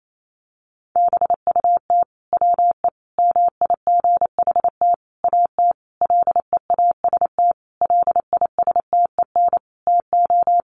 Morse code, heard as a keyed tone, about 28 wpm, 700 Hz. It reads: BUT WE MIGHT AT LEAST LISTEN TO
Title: Sentences from Top 1300 Words 28wpm